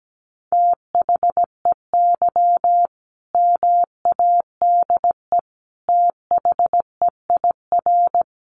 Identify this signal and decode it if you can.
Morse code, heard as a keyed tone, about 17 wpm, 700 Hz. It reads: THEY MADE THEIR
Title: Sentences from Top 1100 Words 17wpm